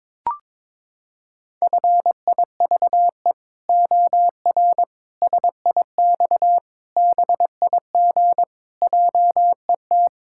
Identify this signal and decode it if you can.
Morse code, heard as a keyed tone, about 22 wpm, 700 Hz. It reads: FIVE OR SIX BIG JET
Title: Pangram Challenge v2 - 22wpm